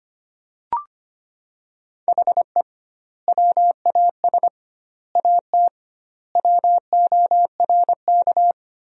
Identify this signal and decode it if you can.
Morse code, heard as a keyed tone, about 25 wpm, 700 Hz. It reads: HE WAS AT WORK
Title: 2x Word-Spacing - Sentences from Top 200 Words - 25wpm